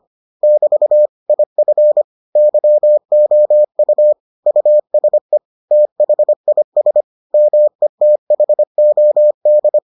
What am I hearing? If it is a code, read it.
Morse code, heard as a keyed tone, about 25 wpm, 600 Hz. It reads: <BT> IF YOU USE THIS METHOD